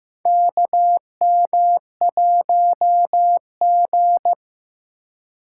Morse code, heard as a keyed tone, about 15 wpm, 700 Hz. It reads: KM1G
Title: USA Call Signs - 4 Characters - 2x Character Spacing - 15wpm